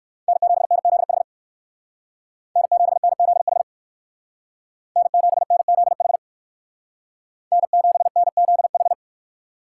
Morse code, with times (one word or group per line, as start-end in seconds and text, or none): 0.3-1.2 N7NZH
2.6-3.6 N7NZH
5.0-6.2 N7NZH
7.5-8.9 N7NZH